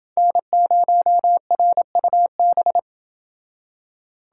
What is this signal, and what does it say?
Morse code, heard as a keyed tone, about 29 wpm, 700 Hz. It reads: N0RUB